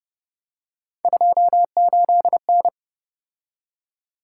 Morse code, heard as a keyed tone, about 30 wpm, 700 Hz. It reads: 28N
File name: Speed-Racing - Three Letter-Number Combinations - 25wpm